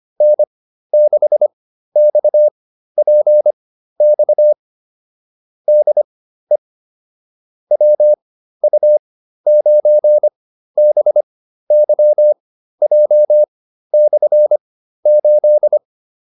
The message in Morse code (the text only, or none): N6XPX DE WU9BYJ/8